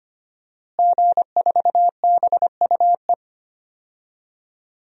Morse code, heard as a keyed tone, about 25 wpm, 700 Hz. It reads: G4BUE
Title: DXCC-100 Call Signs - 5 Characters - Repeated 3x - 25wpm